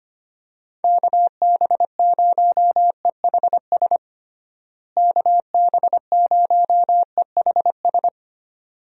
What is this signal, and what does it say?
Morse code, heard as a keyed tone, about 25 wpm, 700 Hz. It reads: KB0EHS KB0EHS
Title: USA Call Signs - 6 Characters - Repeated 2x - 25wpm